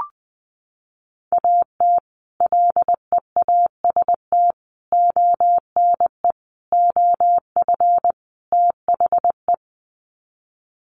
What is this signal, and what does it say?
Morse code, heard as a keyed tone, about 20 wpm, 700 Hz. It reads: AT LEAST ONE OF THE
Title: Sets of 5 Words Encore 20wpm